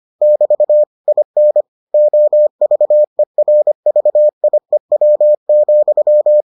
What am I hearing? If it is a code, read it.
Morse code, heard as a keyed tone, about 25 wpm, 600 Hz. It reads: <BT> IN OVERVIEW,